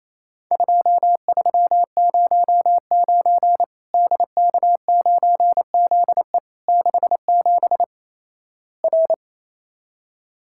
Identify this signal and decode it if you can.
Morse code, heard as a keyed tone, about 28 wpm, 700 Hz. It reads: 2309 DK9ZE 67 R
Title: Worked All Europe DX Contest - QTC Exchange - 28wpm